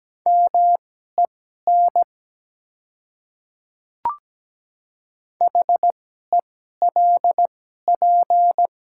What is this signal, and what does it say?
Morse code, heard as a keyed tone, about 17 wpm, spaced as for 12 wpm, 690 Hz. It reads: MEN HELP
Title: 2x Character Spacing - Top 300 Words - 17wpm